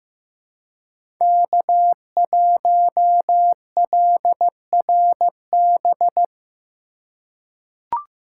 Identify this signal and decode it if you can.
Morse code, heard as a keyed tone, about 15 wpm, 700 Hz. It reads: K1LRB
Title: USA Call Signs - 5 Characters - 4x Character Spacing - 15wpm